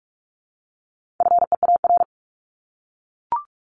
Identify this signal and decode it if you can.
Morse code, heard as a keyed tone, about 44 wpm, 700 Hz. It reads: FEAR
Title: Top 1751-2000 Words 45wpm